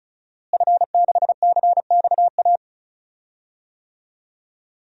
Morse code, heard as a keyed tone, about 36 wpm, 700 Hz. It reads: F6CXA
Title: DXCC-100 Call Signs - 5 Characters - Repeated 3x - 35wpm